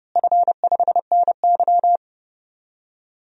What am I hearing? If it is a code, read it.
Morse code, heard as a keyed tone, about 30 wpm, 700 Hz. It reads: F5NY
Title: DXCC-100 Call Signs - 4 Characters - Repeated 2x - 30wpm